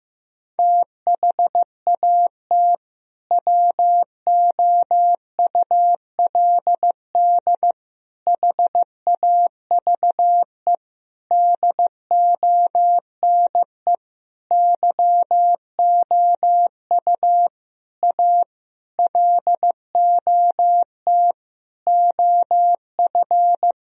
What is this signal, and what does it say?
Morse code, heard as a keyed tone, about 15 wpm, 700 Hz. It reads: THAT WOULD HAVE DONE YOU A LOT OF
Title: Sentences from Top 1000 Words - Review Part 3 - 15wpm